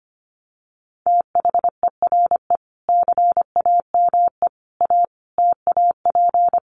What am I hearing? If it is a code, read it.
Morse code, heard as a keyed tone, about 25 wpm, 700 Hz. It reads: THERE CAME A TAP
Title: Sentences from Top 1400 Words 25wpm